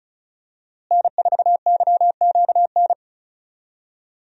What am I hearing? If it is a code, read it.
Morse code, heard as a keyed tone, about 36 wpm, 700 Hz. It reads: N4YQN
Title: USA Call Signs - 5 Characters - Repeated 3x - 35wpm